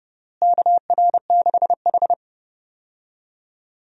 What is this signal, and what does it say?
Morse code, heard as a keyed tone, about 30 wpm, 700 Hz. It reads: KR6H
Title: USA Call Signs - 4 Characters - Repeated 3x - 30wpm